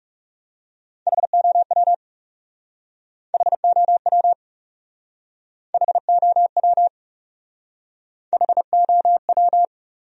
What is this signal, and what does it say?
Morse code, heard as a keyed tone, about 40 wpm, 700 Hz. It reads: HOW HOW HOW HOW